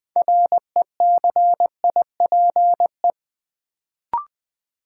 Morse code, heard as a keyed tone, about 20 wpm, 700 Hz. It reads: RECIPE
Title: Sentences from Top 1200 Words - New Word Drill - 20wpm